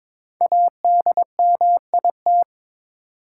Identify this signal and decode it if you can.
Morse code, heard as a keyed tone, about 22 wpm, 700 Hz. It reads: ADMIT
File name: 5 Letter Words - First 3 Letters Given - 22wpm